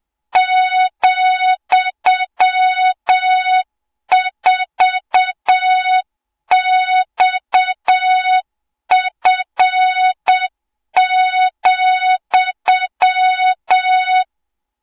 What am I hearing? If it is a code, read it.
Morse code, heard as a keyed tone, about 7 wpm, 3700 Hz, with noise. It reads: ,4XF,
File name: Morze-HA2MN / hang / 20 GYAKORLAT